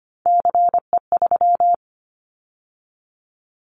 Morse code, heard as a keyed tone, about 25 wpm, 700 Hz. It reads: CE3
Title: Callsign Prefixes for Top 100 DXCC - 25wpm